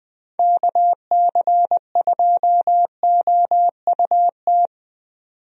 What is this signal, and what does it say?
Morse code, heard as a keyed tone, about 20 wpm, 700 Hz. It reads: KC2OUT
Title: USA Call Signs - 6 Characters - 20wpm